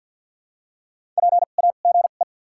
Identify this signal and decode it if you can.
Morse code, heard as a keyed tone, about 52 wpm, spaced as for 37 wpm, 700 Hz. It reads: PAGE
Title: 2x Character Spacing - Top 300 Words - 50wpm